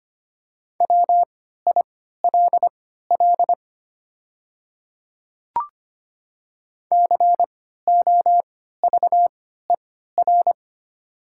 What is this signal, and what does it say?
Morse code, heard as a keyed tone, about 25 wpm, spaced as for 14 wpm, 690 Hz. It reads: WILL COVER
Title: 3x Character Spacing - Top 500 Words - 25wpm